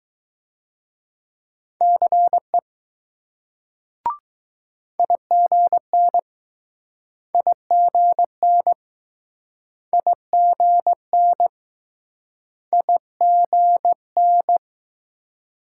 Morse code, heard as a keyed tone, about 20 wpm, 700 Hz. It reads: CE IGN IGN IGN IGN